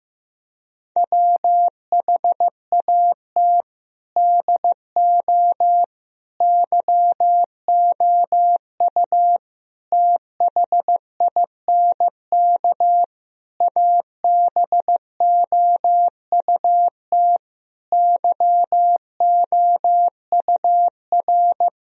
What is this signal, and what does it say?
Morse code, heard as a keyed tone, about 15 wpm, 700 Hz. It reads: WHAT DO YOU THINK ABOUT YOUR
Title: Sentences from Top 1000 Words - Review Part 4 - 15wpm